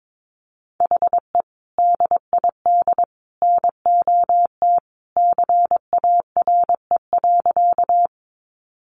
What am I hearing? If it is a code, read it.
Morse code, heard as a keyed tone, about 22 wpm, 700 Hz. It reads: HE DID NOT CARE.
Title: Sentences from Top 600 Words 22wpm